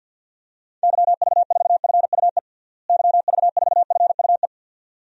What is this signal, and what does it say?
Morse code, heard as a keyed tone, about 52 wpm, 700 Hz. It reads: YV4FUE YV4FUE
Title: DXCC-100 Call Signs - 6 Characters - Repeated 2x - 50wpm